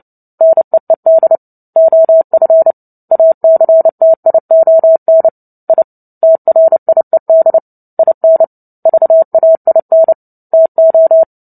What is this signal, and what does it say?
Morse code, heard as a keyed tone, about 30 wpm, 650 Hz. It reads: NEED OF ACTION I TRIED IN VAIN TO
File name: Book Two / 03-Chapter 3 - 29 WPM